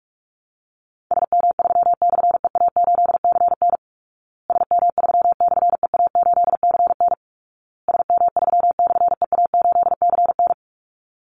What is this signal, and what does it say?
Morse code, heard as a keyed tone, about 44 wpm, 700 Hz. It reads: SM3/EA8CN SM3/EA8CN SM3/EA8CN
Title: Call Signs Repeated 2x - 45wpm